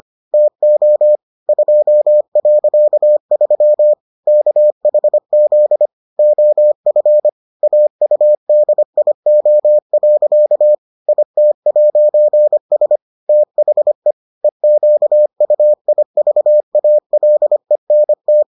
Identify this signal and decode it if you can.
Morse code, heard as a keyed tone, about 25 wpm, 600 Hz. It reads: TO 2.3 KHZ OF AUDIO. IT'S THE EQUIVALENT